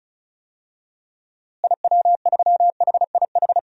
Morse code, heard as a keyed tone, about 36 wpm, 700 Hz. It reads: IW3HIH
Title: DXCC-100 Call Signs - 6 Characters - Repeated 2x - 35wpm